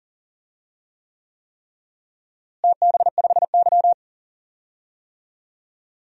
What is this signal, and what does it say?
Morse code, heard as a keyed tone, about 40 wpm, 700 Hz. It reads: TB5Y
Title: DXCC-100 Call Signs - 4 Characters - Repeated 3x - 40wpm